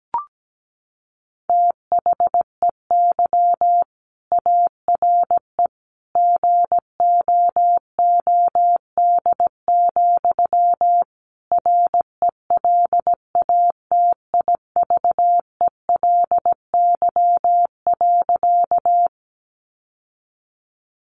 Morse code, heard as a keyed tone, about 17 wpm, 700 Hz. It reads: THEY ARE GOOD, RELATIVELY.
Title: Sentences from Top 1100 Words 17wpm